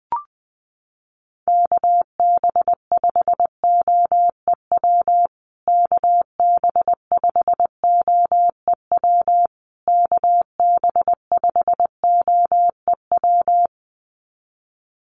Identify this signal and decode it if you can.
Morse code, heard as a keyed tone, about 20 wpm, 700 Hz. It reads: KB5OEW KB5OEW KB5OEW
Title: USA Call Signs Repeated 3x 20wpm